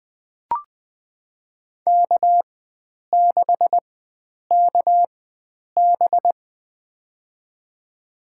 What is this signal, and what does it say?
Morse code, heard as a keyed tone, about 20 wpm, spaced as for 9 wpm, 700 Hz. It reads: K6KB
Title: USA Call Signs - 4 Characters - 4x Character Spacing - 20wpm